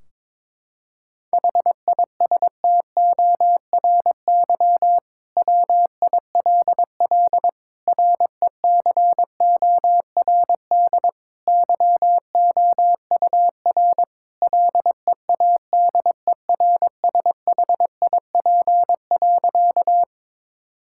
Morse code, heard as a keyed tone, about 21 wpm, 700 Hz. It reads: HISTORY WILL RECORD YOUR LEADERSHIP.
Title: Sentences from Top 1600 Words 22wpm